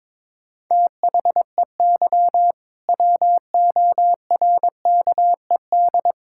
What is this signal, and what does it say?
Morse code, heard as a keyed tone, about 22 wpm, 700 Hz. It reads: THEY WORKED
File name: Difficult Sentences with Hints - 22wpm